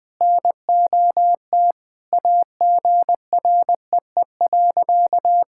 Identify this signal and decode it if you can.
Morse code, heard as a keyed tone, about 20 wpm, 700 Hz. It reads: NOT AGREE.
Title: Sentences with 4-Word Phrases - 20wpm